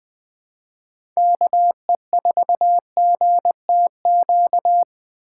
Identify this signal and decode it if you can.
Morse code, heard as a keyed tone, about 20 wpm, 700 Hz. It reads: KE4GTQ
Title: 2023 POTA Activator Call Signs - 20wpm